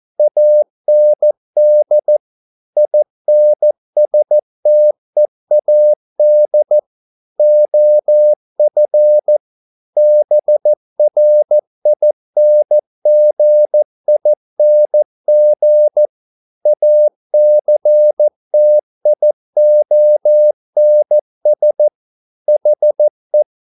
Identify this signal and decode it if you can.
Morse code, heard as a keyed tone, about 14 wpm, 600 Hz. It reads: AND INSTEAD OF BRINGING ACTIONS HE
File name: mp3 files / a dogs mistake - 14wpm